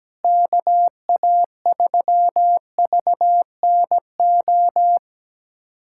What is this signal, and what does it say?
Morse code, heard as a keyed tone, about 17 wpm, 700 Hz. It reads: KA3VNO
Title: USA Callsign DE Callsign - 6 Characters - 17wpm